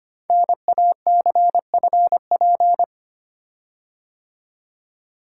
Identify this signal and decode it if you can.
Morse code, heard as a keyed tone, about 25 wpm, 700 Hz. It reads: NACFP
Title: Code Groups - 5 Letter:Numbers - 25wpm